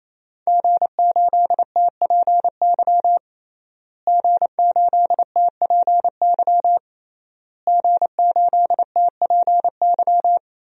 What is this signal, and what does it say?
Morse code, heard as a keyed tone, about 28 wpm, 700 Hz. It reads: G8TPY G8TPY G8TPY
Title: DXCC-100 Call Signs - 5 Characters - Repeated 3x - 28wpm